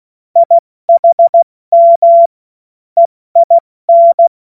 Morse code, heard as a keyed tone, about 16 wpm, 680 Hz. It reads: IHM EIN
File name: Morsekurs / Witze / Witz 02. Februar 2021, 70 F80